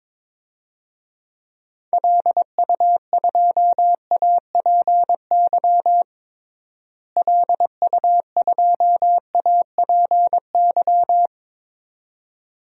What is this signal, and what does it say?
Morse code, heard as a keyed tone, about 22 wpm, 700 Hz. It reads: LU2APY LU2APY
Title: DXCC-100 Call Signs - 6 Characters - Repeated 2x - 22wpm